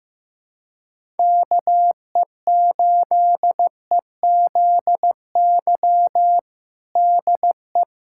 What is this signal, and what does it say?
Morse code, heard as a keyed tone, about 16 wpm, 700 Hz. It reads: KE8EZY DE
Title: USA Callsign DE Callsign - 6 Characters - 15wpm